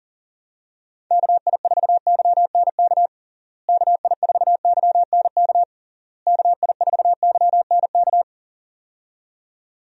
Morse code, heard as a keyed tone, about 40 wpm, 700 Hz. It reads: KI4YNK KI4YNK KI4YNK
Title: USA Call Signs - 6 Characters - Repeated 3x - 40wpm